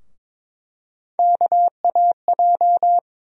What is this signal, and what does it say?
Morse code, heard as a keyed tone, about 21 wpm, 700 Hz. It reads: KAJ
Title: USA Call Sign Postfixes - 3 Letters 22wpm